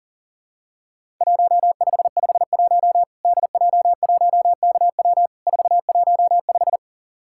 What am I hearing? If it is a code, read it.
Morse code, heard as a keyed tone, about 44 wpm, 700 Hz. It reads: 1551 DJ1KW 415
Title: Worked All Europe DX Contest - QTC Exchange - 40wpm